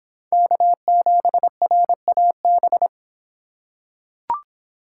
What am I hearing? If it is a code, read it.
Morse code, heard as a keyed tone, about 26 wpm, 700 Hz. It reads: K7RAB